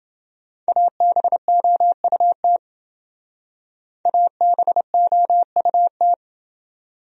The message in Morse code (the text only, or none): ABOUT ABOUT